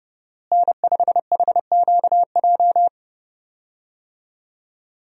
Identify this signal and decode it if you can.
Morse code, heard as a keyed tone, about 30 wpm, 700 Hz. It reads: N5HQJ
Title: USA Call Signs - 5 Characters - Repeated 3x - 30wpm